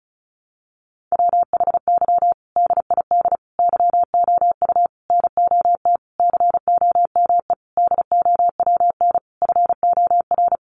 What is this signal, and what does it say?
Morse code, heard as a keyed tone, about 34 wpm, 700 Hz. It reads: WHY DID YOU NOT COME DOWN FOR
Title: Sentences from Top 1500 Words 35wpm